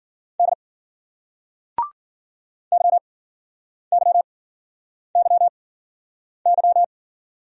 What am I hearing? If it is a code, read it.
Morse code, heard as a keyed tone, about 56 wpm, spaced as for 9 wpm, 700 Hz. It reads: D YYYY